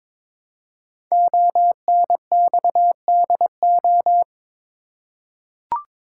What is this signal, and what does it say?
Morse code, heard as a keyed tone, about 22 wpm, 700 Hz. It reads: ONXDO
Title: Code Groups - 5 Letters - 22wpm